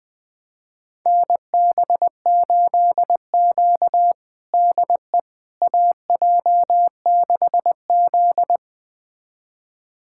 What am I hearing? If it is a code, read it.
Morse code, heard as a keyed tone, about 20 wpm, 700 Hz. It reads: NB8Q DE AJ6Z
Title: USA Callsign DE Callsign - 4 Characters - 20wpm